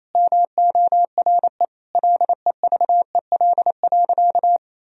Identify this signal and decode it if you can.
Morse code, heard as a keyed tone, about 28 wpm, 700 Hz. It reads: MORE LEVEL.
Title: Sentences from Top 1000 Words - Review Part 4 - 28wpm